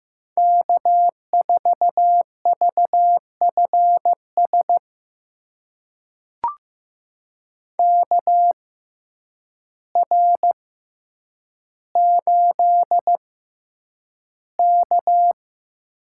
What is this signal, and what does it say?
Morse code, heard as a keyed tone, about 15 wpm, spaced as for 8 wpm, 700 Hz. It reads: K4VFS KR8K
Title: USA Call Signs - 6x Character Spacing - 15wpm